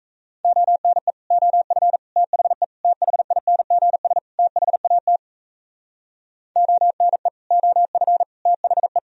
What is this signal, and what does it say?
Morse code, heard as a keyed tone, about 35 wpm, 700 Hz. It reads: ONE OF THE THINGS THAT ONE OF THE